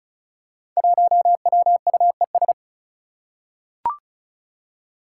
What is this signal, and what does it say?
Morse code, heard as a keyed tone, about 36 wpm, 700 Hz. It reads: 1WUES
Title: Five Challenging Letter-Number Combinations 35wpm